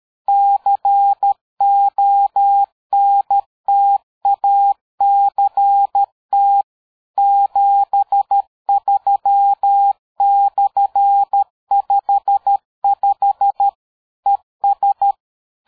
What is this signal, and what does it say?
Morse code, heard as a keyed tone, about 13 wpm, 800 Hz. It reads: CONTACT 73/55 ES
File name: dk5ke / QSO 60BpM